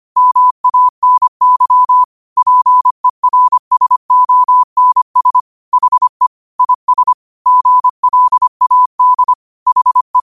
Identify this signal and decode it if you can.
Morse code, heard as a keyed tone, about 25 wpm, 1000 Hz. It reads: MANY PERSONS HE IS GLAD HE
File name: qotdfiles / 25wpm